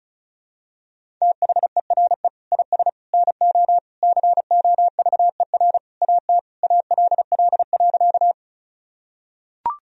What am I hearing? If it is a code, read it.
Morse code, heard as a keyed tone, about 34 wpm, 700 Hz. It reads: THERE IS NO COVER AT ALL.
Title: Sentences from Top 1000 Words - Review Part 1 - 35wpm